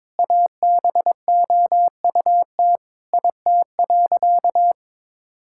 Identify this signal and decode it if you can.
Morse code, heard as a keyed tone, about 22 wpm, 700 Hz. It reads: ABOUT IT.
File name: Sentences with 7-Word Phrases - 22wpm